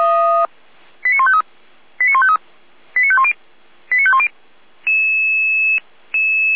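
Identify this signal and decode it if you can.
Five-tone selective call (ZVEI), heard as quick runs of five tones, tones of about 70 ms. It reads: 89153 89153 89510 89510
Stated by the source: ZVEI five-tone call 89160 received on SDR, from the feuerwehr-orlamuende.org server, a fire brigade.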